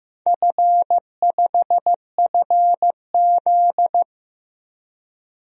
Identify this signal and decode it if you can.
Morse code, heard as a keyed tone, about 15 wpm, 700 Hz. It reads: F5FZ
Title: DXCC-100 Call Signs - 4 Characters - Repeated 2x - 15wpm